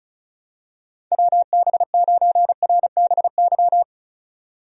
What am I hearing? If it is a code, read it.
Morse code, heard as a keyed tone, about 34 wpm, 700 Hz. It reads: WB9RBY
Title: USA Call Signs - 6 Characters - 35wpm